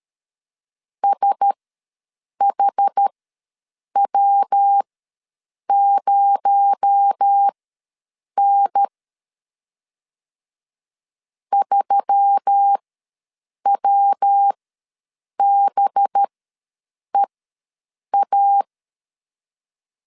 Morse code, heard as a keyed tone, about 13 wpm, spaced as for 7 wpm, 800 Hz. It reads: SHW0N 3WBEA
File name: LEZIONI / 6 PKX0 / 6 PKX0 ws 12wpm 3448